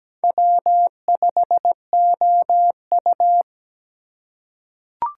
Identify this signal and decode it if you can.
Morse code, heard as a keyed tone, about 17 wpm, 700 Hz. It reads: W5OU
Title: USA Call Signs - 6x Character Spacing - 17wpm